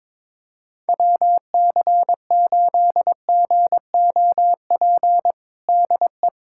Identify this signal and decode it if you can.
Morse code, heard as a keyed tone, about 22 wpm, 700 Hz. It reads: WC8GOP DE
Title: USA Callsign DE Callsign 22wpm